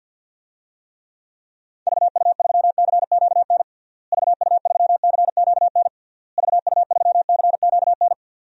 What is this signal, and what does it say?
Morse code, heard as a keyed tone, about 49 wpm, 700 Hz. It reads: VU3CQN VU3CQN VU3CQN
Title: DXCC-100 Call Signs - 6 Characters - Repeated 3x - 50wpm